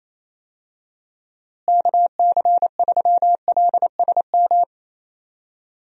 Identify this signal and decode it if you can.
Morse code, heard as a keyed tone, about 28 wpm, 700 Hz. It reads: KC3LSM
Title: USA Call Signs - 6 Characters - Repeated 2x - 28wpm